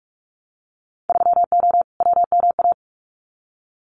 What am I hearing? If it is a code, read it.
Morse code, heard as a keyed tone, about 46 wpm, 700 Hz. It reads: 3O WMA
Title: Winter Field Day Exchanges Encore 45wpm